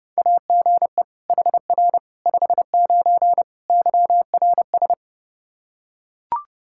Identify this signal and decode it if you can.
Morse code, heard as a keyed tone, about 30 wpm, 700 Hz. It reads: AGE HR 59 YRS
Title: Practice QSO - Exchange 3 - More Info - 30wpm